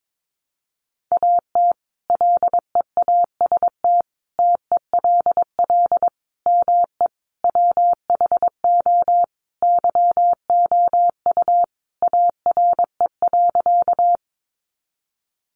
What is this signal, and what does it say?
Morse code, heard as a keyed tone, about 22 wpm, 700 Hz. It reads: AT LEAST TELL ME WHO YOU ARE.
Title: Sentences from Top 700 Words 22wpm